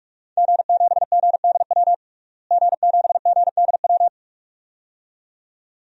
Morse code, heard as a keyed tone, about 46 wpm, 700 Hz. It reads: G7GDW G7GDW
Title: DXCC-100 Call Signs - 5 Characters - Repeated 2x - 45wpm